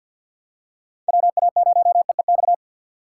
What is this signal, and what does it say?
Morse code, heard as a keyed tone, about 47 wpm, 700 Hz. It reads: WA0EEX
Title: USA Call Signs - 6 Characters - 4x Character Spacing - 50wpm